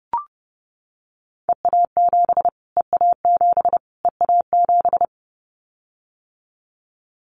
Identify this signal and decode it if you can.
Morse code, heard as a keyed tone, about 28 wpm, 700 Hz. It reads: EA7 EA7 EA7
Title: Callsign Prefixes for Top 25 DXCC - Repeated 3x - 30wpm